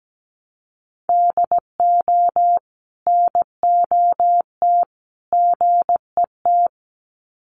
Morse code, heard as a keyed tone, about 17 wpm, 700 Hz. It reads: DO NOT GET
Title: Sets of 3 Words 17wpm